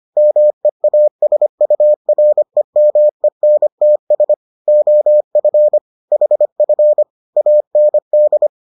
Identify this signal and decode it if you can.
Morse code, heard as a keyed tone, about 25 wpm, 600 Hz. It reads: MEASUREMENTS OF HF AND